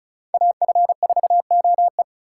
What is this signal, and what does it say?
Morse code, heard as a keyed tone, about 36 wpm, 700 Hz. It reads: AF4OE
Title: USA Call Signs - 5 Characters - 4x Character Spacing - 35wpm